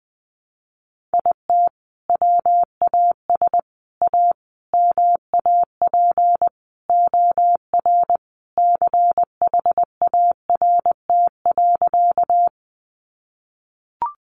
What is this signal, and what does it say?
Morse code, heard as a keyed tone, about 20 wpm, 700 Hz. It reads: IT WAS A MAP OR CHART.
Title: Sentences from Top 1300 Words 20wpm